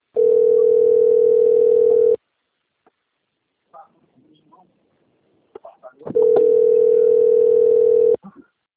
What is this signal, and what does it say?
Ringback tone (North America), 440 + 480 Hz, with speech.
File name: rg-1600-unknown- - - .6529